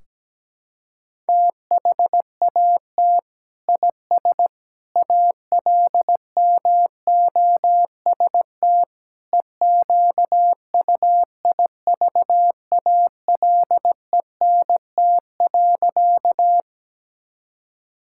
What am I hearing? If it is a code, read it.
Morse code, heard as a keyed tone, about 17 wpm, 700 Hz. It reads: THAT IS ALMOST EQUIVALENT.
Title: Sentences from Top 1600 Words 17wpm